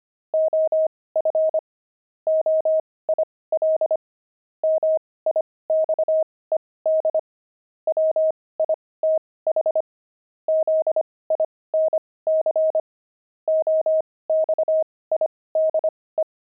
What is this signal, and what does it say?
Morse code, heard as a keyed tone, about 25 wpm, spaced as for 18 wpm, 630 Hz. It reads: OF OIL MIXED WITH ZINC OXIDE
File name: pangram-fs-18-cs-25-oxide